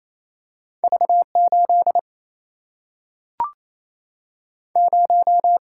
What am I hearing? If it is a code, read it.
Morse code, heard as a keyed tone, about 28 wpm, 700 Hz. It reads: V 8 0
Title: Single-Letter Single-Number 28wpm